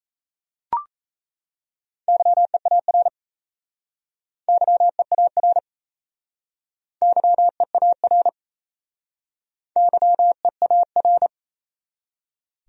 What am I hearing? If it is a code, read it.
Morse code, heard as a keyed tone, about 35 wpm, 700 Hz. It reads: YEAR YEAR YEAR YEAR
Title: Speed-Racing Top 300 Words 28wpm